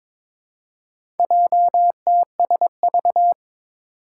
Morse code, heard as a keyed tone, about 22 wpm, 700 Hz. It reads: JTSV
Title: Four Challenging Letter-Number Combinations 22wpm